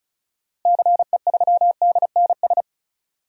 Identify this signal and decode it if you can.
Morse code, heard as a keyed tone, about 36 wpm, 700 Hz. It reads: CE3DNS
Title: DXCC-100 Call Signs - 6 Characters - Repeated 2x - 35wpm